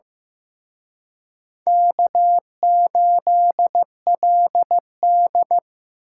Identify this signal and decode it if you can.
Morse code, heard as a keyed tone, about 18 wpm, 700 Hz. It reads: K8LD
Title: Speed Racing - USA 1x2 Call Signs 15wpm